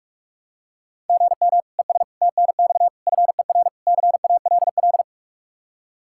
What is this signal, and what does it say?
Morse code, heard as a keyed tone, about 46 wpm, 700 Hz. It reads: GM ES TNX FER CALL
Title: Practice QSO - Exchange 1 - Beginning - 45wpm